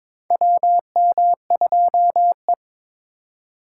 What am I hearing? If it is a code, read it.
Morse code, heard as a keyed tone, about 22 wpm, 700 Hz. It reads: WM2E
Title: USA Call Signs - 4 Characters - 4x Character Spacing - 22wpm